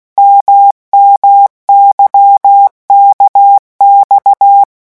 Morse code, heard as a keyed tone, about 16 wpm, 800 Hz. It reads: MMYKX
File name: morse mp3 / revision2-vitesse 15 mots